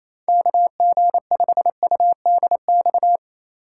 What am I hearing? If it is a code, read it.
Morse code, heard as a keyed tone, about 28 wpm, 700 Hz. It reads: KG5UDX
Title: USA Call Signs - 4x Character Spacing - 28wpm